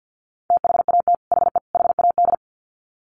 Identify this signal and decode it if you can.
Morse code, heard as a keyed tone, about 50 wpm, 700 Hz. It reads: THAT HE HAD
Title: Sets of 3 Words 50wpm